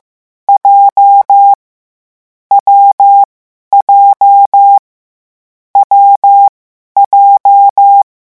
Morse code, heard as a keyed tone, about 15 wpm, 800 Hz. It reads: J W J W J
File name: morse mp3 / lecon7-vitesse 14